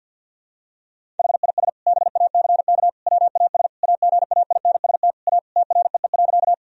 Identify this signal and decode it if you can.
Morse code, heard as a keyed tone, about 52 wpm, 700 Hz. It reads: HIS BACK WAS AGAINST A TREE.
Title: Sentences from Top 1000 Words - Review Part 1 - 50wpm